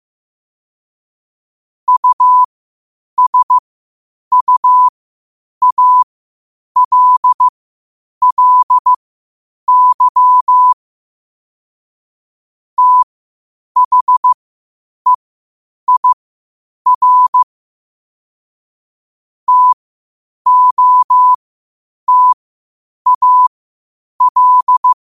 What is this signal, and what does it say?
Morse code, heard as a keyed tone, about 15 wpm, spaced as for 8 wpm, 1000 Hz. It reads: USUALLY THEIR TOTAL
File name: qotdfiles / 8wpm